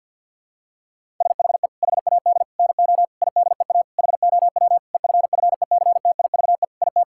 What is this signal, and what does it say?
Morse code, heard as a keyed tone, about 50 wpm, 700 Hz. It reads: SHE HAD NO IDEA HOW EFFECTIVE IT
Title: Sentences from Top 1000 Words - Review Part 6 - 50wpm